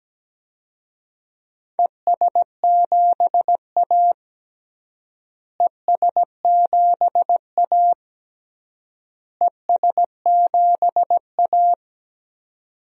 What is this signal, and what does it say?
Morse code, heard as a keyed tone, about 17 wpm, 700 Hz. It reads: ES7A ES7A ES7A
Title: DXCC-100 Call Signs - 4 Characters - Repeated 3x - 17wpm